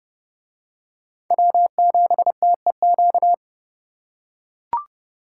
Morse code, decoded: W7TEQ

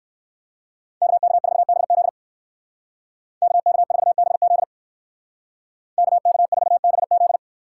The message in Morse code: KK4BZ KK4BZ KK4BZ

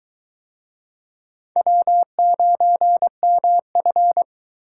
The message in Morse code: W9MF

